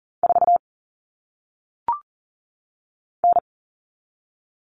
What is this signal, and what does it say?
Morse code, heard as a keyed tone, about 40 wpm, 700 Hz. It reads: 4 N
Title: All Cut Numbers - 1-Digit 40wpm